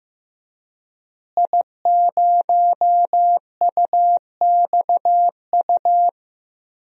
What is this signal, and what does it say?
Morse code, heard as a keyed tone, about 15 wpm, 700 Hz. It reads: I0UXU
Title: DXCC-100 Call Signs - 5 Characters - Repeated 2x - 15wpm